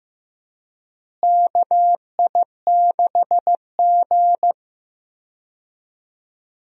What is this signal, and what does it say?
Morse code, heard as a keyed tone, about 15 wpm, 700 Hz. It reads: KI6G
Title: USA Call Signs - 4 Characters - Repeated 3x - 15wpm